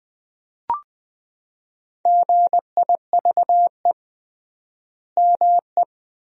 Morse code, heard as a keyed tone, about 20 wpm, 700 Hz. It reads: GIVE ME
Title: 3x Word-Spacing - Sentences from Top 200 Words - 20wpm